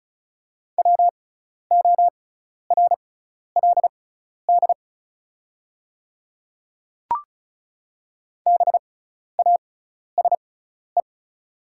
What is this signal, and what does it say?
Morse code, heard as a keyed tone, about 36 wpm, spaced as for 12 wpm, 690 Hz. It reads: WORLD BASE
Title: 6x Character Spacing - Top 300 Words - 35wpm